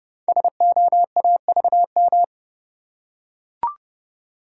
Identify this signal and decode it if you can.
Morse code, heard as a keyed tone, about 30 wpm, 700 Hz. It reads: SOAVM
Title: Code Groups - 5 Letters - 30wpm